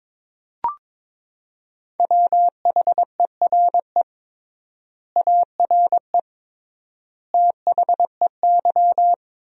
Morse code, heard as a keyed tone, about 21 wpm, 700 Hz. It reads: WHERE ARE THEY
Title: 3x Word-Spacing - Sentences from Top 200 Words - 22wpm